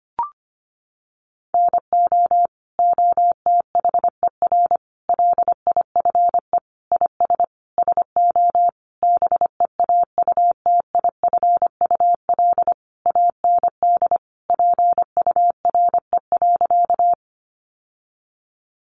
Morse code, heard as a keyed tone, about 25 wpm, 700 Hz. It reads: NO OTHER LIFE IS SO BEAUTIFUL AND PURE.
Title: Sentences from Top 1500 Words 25wpm